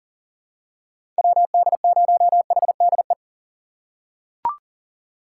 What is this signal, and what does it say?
Morse code, heard as a keyed tone, about 40 wpm, 700 Hz. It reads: WD0HDE
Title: USA Call Signs - 3x Character Spacing - 40wpm